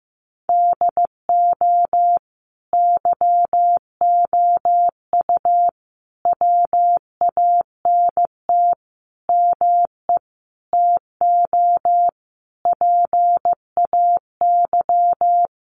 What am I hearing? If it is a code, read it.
Morse code, heard as a keyed tone, about 15 wpm, 700 Hz. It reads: DO YOU WANT ME TO PAY
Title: Sentences from Top 1400 Words 15wpm